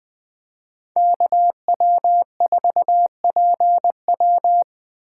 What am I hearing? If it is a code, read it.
Morse code, heard as a keyed tone, about 20 wpm, 700 Hz. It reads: KW4PW
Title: Speed Racing - USA 2x2 Call Signs 17wpm